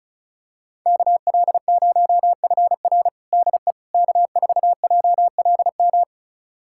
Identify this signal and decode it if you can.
Morse code, heard as a keyed tone, about 36 wpm, 700 Hz. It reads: KL0FR DE K4JLM
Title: USA Callsign DE Callsign - 5 Characters - 35wpm